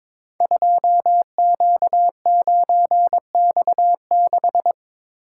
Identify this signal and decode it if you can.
Morse code, heard as a keyed tone, about 23 wpm, 700 Hz. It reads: 2Q9X6
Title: Code Groups - 5 Letter:Numbers - 22wpm